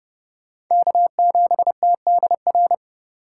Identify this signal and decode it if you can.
Morse code, heard as a keyed tone, about 28 wpm, 700 Hz. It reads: K7TDR